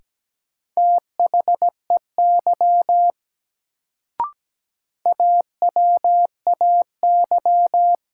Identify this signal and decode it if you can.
Morse code, heard as a keyed tone, about 17 wpm, 700 Hz. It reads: THEY AWAY
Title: Sets of 2 Words - New Word Drill - 17wpm